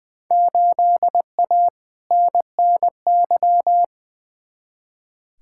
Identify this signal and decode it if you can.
Morse code, heard as a keyed tone, about 20 wpm, 700 Hz. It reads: 8A NNY
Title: Field Day Exchanges Encore - 20wpm